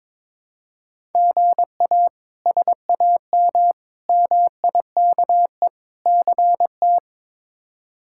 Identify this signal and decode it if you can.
Morse code, heard as a keyed tone, about 22 wpm, 700 Hz. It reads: GA SAM MIKE CT
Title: K1USN SST QSO Practice - 22wpm